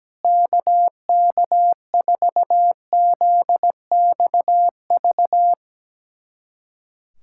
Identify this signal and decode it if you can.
Morse code, heard as a keyed tone, about 17 wpm, 700 Hz. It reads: KK4ZXV
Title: USA Callsign DE Callsign 17wpm